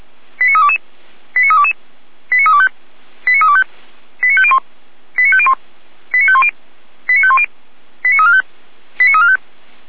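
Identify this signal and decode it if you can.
Five-tone selective call (ZVEI), heard as quick runs of five tones, tones of about 80 ms. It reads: 89322 89322 89325 89325 89501 89501 89510 89510 89345 89345
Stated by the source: ZVEI five-tone selective call 89340, received with an SDR, from the Feuerwehr Orlamünde audio server.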